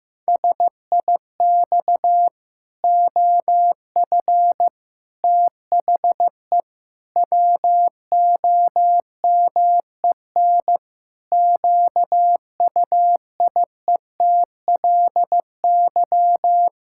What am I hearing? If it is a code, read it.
Morse code, heard as a keyed tone, about 16 wpm, 700 Hz. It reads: SIX OF THE WOMEN QUIETLY
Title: Pangram Challenge v2 - 15wpm